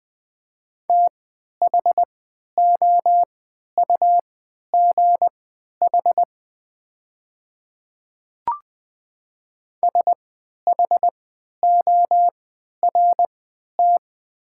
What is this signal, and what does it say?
Morse code, heard as a keyed tone, about 20 wpm, spaced as for 11 wpm, 700 Hz. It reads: THOUGH SHORT
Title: 3x Character Spacing - Top 500 Words - 20wpm